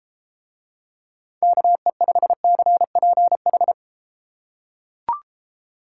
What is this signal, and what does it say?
Morse code, heard as a keyed tone, about 33 wpm, 700 Hz. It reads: KE5CPH